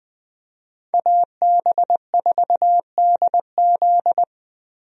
Morse code, heard as a keyed tone, about 21 wpm, 700 Hz. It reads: AB4DZ
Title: USA Call Signs - 5 Characters - 20wpm